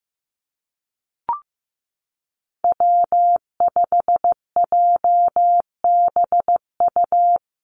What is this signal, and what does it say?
Morse code, heard as a keyed tone, about 15 wpm, 700 Hz. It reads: W5JBU
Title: USA Call Signs Repeated 3x 15wpm